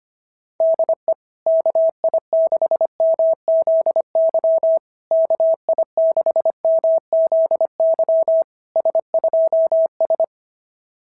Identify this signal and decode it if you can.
Morse code, heard as a keyed tone, about 24 wpm, 650 Hz. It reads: DE KI6MZY KI6MZY S2S
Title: SOTA QSO Practice 25wpm